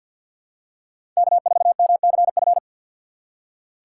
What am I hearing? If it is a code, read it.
Morse code, heard as a keyed tone, about 47 wpm, 700 Hz. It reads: K4MCF
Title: USA Call Signs - 5 Characters - 3x Character Spacing - 50wpm